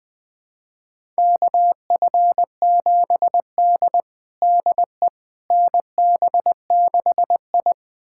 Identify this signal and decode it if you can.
Morse code, heard as a keyed tone, about 20 wpm, 700 Hz. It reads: KF7D DE NB6I
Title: USA Callsign DE Callsign - 4 Characters - 20wpm